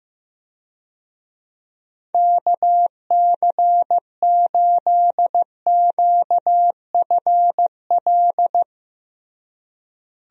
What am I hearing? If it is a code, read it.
Morse code, heard as a keyed tone, about 15 wpm, 700 Hz. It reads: KC8QFL
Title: USA Call Signs - 6 Characters - Repeated 3x - 15wpm